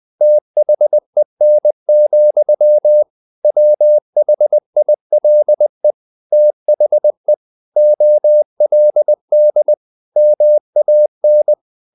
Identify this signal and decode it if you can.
Morse code, heard as a keyed tone, about 20 wpm, 600 Hz. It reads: THEN, WHILE THE OLD MAN
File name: the old man and the sea / the old man and the sea part 1 20wpm